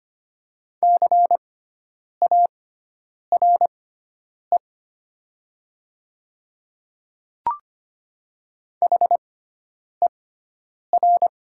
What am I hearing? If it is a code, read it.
Morse code, heard as a keyed tone, about 25 wpm, spaced as for 9 wpm, 700 Hz. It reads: CARE HER
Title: 6x Character Spacing - Top 300 Words - 25wpm